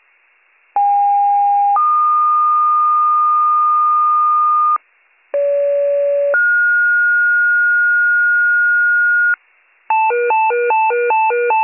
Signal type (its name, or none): two-tone page